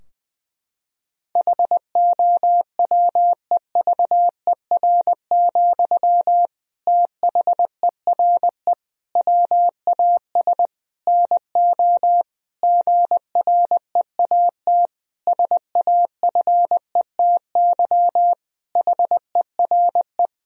Morse code, heard as a keyed tone, about 20 wpm, 700 Hz. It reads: HOWEVER, THERE WAS NO GREAT SAFETY HERE
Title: Sentences from Top 1000 Words - Review Part 4 - 20wpm